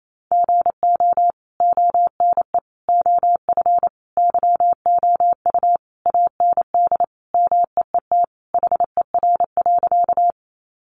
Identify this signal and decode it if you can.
Morse code, heard as a keyed tone, about 28 wpm, 700 Hz. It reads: GO ONE OF YOU AND MEET HER.
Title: Sentences from Top 1100 Words 28wpm